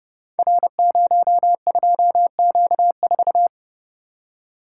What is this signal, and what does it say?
Morse code, heard as a keyed tone, about 30 wpm, 700 Hz. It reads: R02Q4
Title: Code Groups - 5 Letter:Numbers - 30wpm